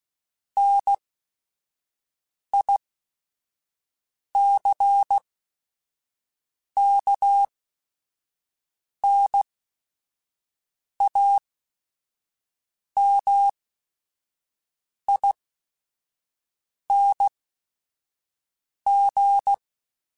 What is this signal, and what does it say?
Morse code, heard as a keyed tone, about 16 wpm, spaced as for 5 wpm, 800 Hz. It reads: NICKNAMING
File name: Technical Subjects / Technical Subjects-mp3s / AND Gate @5